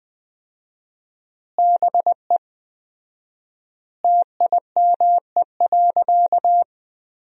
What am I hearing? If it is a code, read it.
Morse code, heard as a keyed tone, about 20 wpm, 700 Hz. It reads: BE TIME.
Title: 4x Word-Spacing - Sentences from Top 200 Words - 20wpm